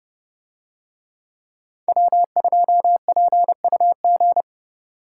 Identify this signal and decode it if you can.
Morse code, heard as a keyed tone, about 30 wpm, 700 Hz. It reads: W2PUG